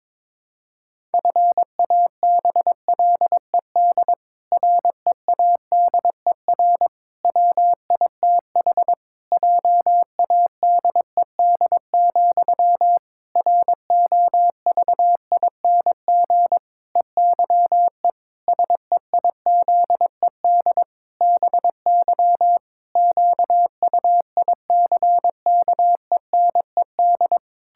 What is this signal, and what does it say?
Morse code, heard as a keyed tone, about 22 wpm, 700 Hz. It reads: FABLED READER WITH JADED, ROVING EYE SEIZED BY QUICKENED
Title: Pangram Challenge v2 - 22wpm